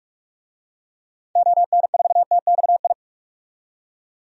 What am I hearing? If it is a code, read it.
Morse code, heard as a keyed tone, about 42 wpm, 700 Hz. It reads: ON4TXI
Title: DXCC-100 Call Signs - 6 Characters - 45wpm